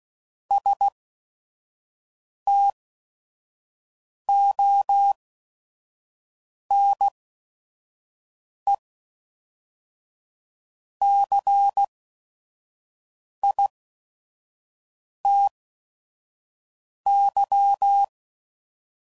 Morse code, heard as a keyed tone, about 16 wpm, spaced as for 5 wpm, 800 Hz. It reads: STONECITY
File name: CW Code Groups @5WPM / fromSM's / Cities